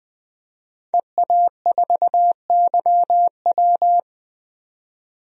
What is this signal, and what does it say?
Morse code, heard as a keyed tone, about 20 wpm, 700 Hz. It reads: EA4YW
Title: DXCC-100 Call Signs - 5 Characters - 20wpm